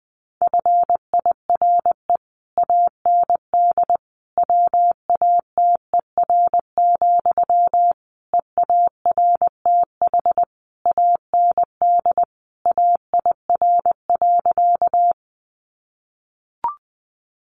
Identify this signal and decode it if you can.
Morse code, heard as a keyed tone, about 20 wpm, 700 Hz. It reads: FIRE AND WATER, EARTH AND AIR.
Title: Sentences from Top 400 Words 20wpm